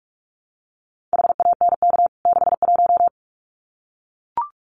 Morse code, heard as a keyed tone, about 44 wpm, 700 Hz. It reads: HANK 61
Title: CWT Exchange 45wpm